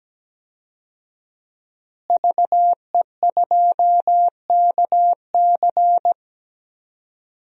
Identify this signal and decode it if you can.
Morse code, heard as a keyed tone, about 17 wpm, 700 Hz. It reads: VE2KC
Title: DXCC-100 Call Signs - 5 Characters - Repeated 3x - 17wpm